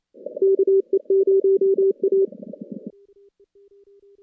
Morse code, heard as a keyed tone, about 28 wpm, 410 Hz. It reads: KE0A
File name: contests / CW-SS-k5tr- -2100